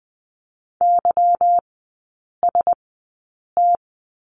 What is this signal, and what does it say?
Morse code, heard as a keyed tone, about 20 wpm, spaced as for 8 wpm, 700 Hz. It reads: YST
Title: 178 Lesson 29e - Slash - Three Characters 20wpm